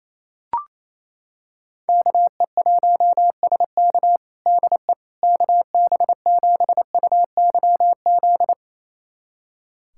Morse code, heard as a keyed tone, about 28 wpm, 700 Hz. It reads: KE1SK DE KB7UYZ
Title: USA Callsign DE Callsign 28wpm